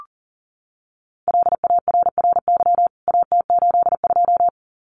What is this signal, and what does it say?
Morse code, heard as a keyed tone, about 40 wpm, 700 Hz. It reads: LARRY AT82
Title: CWT Exchange 40wpm